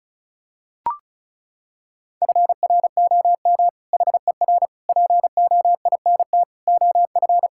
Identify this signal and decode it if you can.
Morse code, heard as a keyed tone, about 36 wpm, 700 Hz. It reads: FROM HER POINT OF
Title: Sentences from Top 1000 Words - Review Part 6 - 35wpm